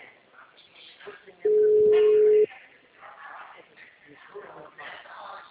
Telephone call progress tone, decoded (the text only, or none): ringback tone